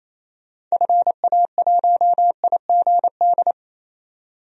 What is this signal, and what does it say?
Morse code, heard as a keyed tone, about 28 wpm, 700 Hz. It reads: FA1IGD
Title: DXCC-100 Call Signs - 6 Characters - 28wpm